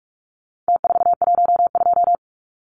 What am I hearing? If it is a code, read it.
Morse code, heard as a keyed tone, about 44 wpm, 700 Hz. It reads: T412
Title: Numbers - Contest Style - 4-Digit 45wpm